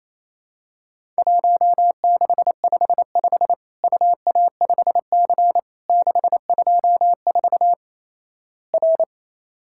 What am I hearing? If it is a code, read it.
Morse code, heard as a keyed tone, about 28 wpm, 700 Hz. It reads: 1655 UA5C 624 R
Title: Worked All Europe DX Contest - QTC Exchange - 28wpm